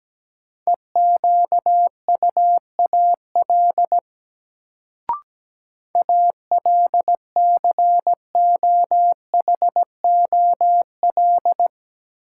Morse code, heard as a keyed tone, about 17 wpm, 700 Hz. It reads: EQUAL ALCOHOL
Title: Sentences from Top 1400 Words - New Word Drill - 17wpm